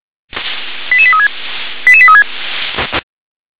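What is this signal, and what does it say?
Five-tone selective call (ZVEI), heard as quick runs of five tones, tones of about 70 ms. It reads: 88836 88836
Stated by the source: ZVEI five-tone call received on SDR.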